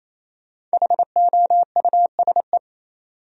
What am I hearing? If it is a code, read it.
Morse code, heard as a keyed tone, about 35 wpm, 700 Hz. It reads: HOUSE
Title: Speed-Racing Top 300 Words 28wpm